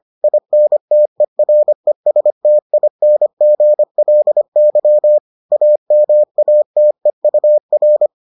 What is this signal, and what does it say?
Morse code, heard as a keyed tone, about 25 wpm, 600 Hz. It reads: INTERESTINGLY AMATEUR